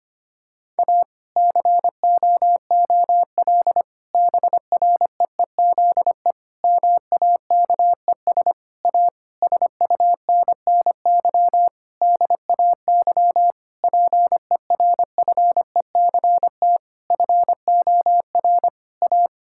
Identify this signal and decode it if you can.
Morse code, heard as a keyed tone, about 25 wpm, 700 Hz. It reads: A COOL BREEZE MAKES A SUNNY DAY PERFECT FOR A
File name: practice / 25wpm / Difficult Sentences with Hints - 25wpm